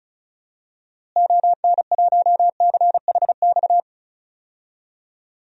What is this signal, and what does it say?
Morse code, heard as a keyed tone, about 34 wpm, 700 Hz. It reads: ON1CHX
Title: DXCC-100 Call Signs - 6 Characters - Repeated 2x - 35wpm